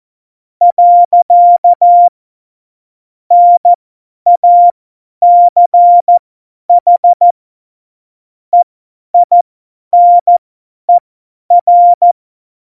Morse code, heard as a keyed tone, about 14 wpm, spaced as for 10 wpm, 700 Hz. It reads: . NACH EINER
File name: Morsekurs / Witze / Witz 02. Januar 2018, 50 F70